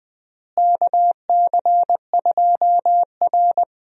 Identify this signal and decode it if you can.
Morse code, heard as a keyed tone, about 20 wpm, 700 Hz. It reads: KC2R
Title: USA Call Signs - 4 Characters - Repeated 2x - 20wpm